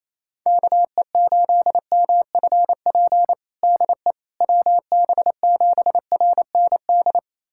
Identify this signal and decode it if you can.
Morse code, heard as a keyed tone, about 28 wpm, 700 Hz. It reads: KE8MFP DE WB7RND
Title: USA Callsign DE Callsign - 6 Characters - 28wpm